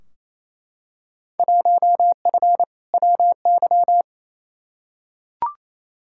Morse code, heard as a keyed tone, about 28 wpm, 700 Hz. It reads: 1F WY
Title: Field Day Exchanges Encore - 28wpm